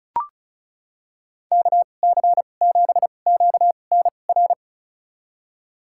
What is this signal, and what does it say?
Morse code, heard as a keyed tone, about 36 wpm, spaced as for 25 wpm, 700 Hz. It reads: KC7QNR
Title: USA Call Signs - 2x Character Spacing - 35wpm